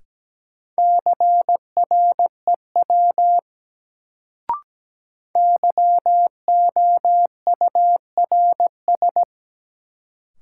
Morse code, heard as a keyed tone, about 17 wpm, 700 Hz. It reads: CREW YOURS
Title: Sentences from Top 1500 Words - New Word Drill - 17wpm